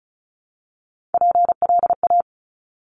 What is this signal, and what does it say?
Morse code, heard as a keyed tone, about 35 wpm, 700 Hz. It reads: PLA
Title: Top Three Letter Combinations in Words 35wpm